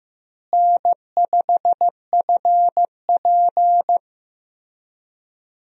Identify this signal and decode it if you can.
Morse code, heard as a keyed tone, about 15 wpm, 700 Hz. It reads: N5FP
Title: USA Call Signs - 4 Characters - Repeated 2x - 15wpm